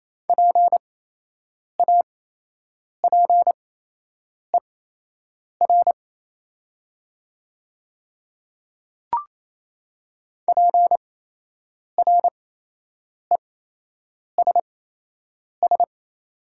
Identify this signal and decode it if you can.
Morse code, heard as a keyed tone, about 28 wpm, spaced as for 8 wpm, 700 Hz. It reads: PAPER PRESS
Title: 8x Character Spacing - Top 300 Words - 28wpm